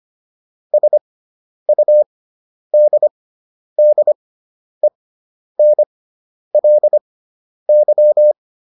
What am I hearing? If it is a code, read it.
Morse code, heard as a keyed tone, about 25 wpm, spaced as for 10 wpm, 600 Hz. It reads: SUDDENLY